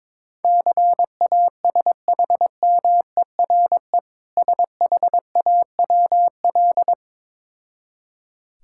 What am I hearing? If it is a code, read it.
Morse code, heard as a keyed tone, about 21 wpm, 690 Hz. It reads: CASHMERE SHAWL
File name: Difficult Words with Hints - 22wpm